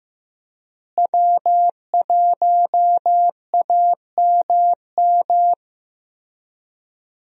Morse code, heard as a keyed tone, about 15 wpm, 700 Hz. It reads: W1AMM
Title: USA Call Signs - 5 Characters - Repeated 3x - 15wpm